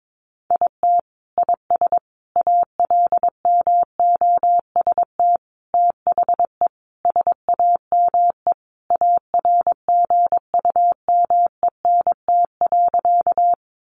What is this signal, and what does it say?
Morse code, heard as a keyed tone, about 22 wpm, 700 Hz. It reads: IT IS ALMOST THE SAME ARGUMENT.
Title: Sentences from Top 1400 Words 22wpm